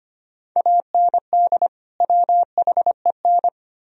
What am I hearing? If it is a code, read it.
Morse code, heard as a keyed tone, about 33 wpm, 700 Hz. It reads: AND WHEN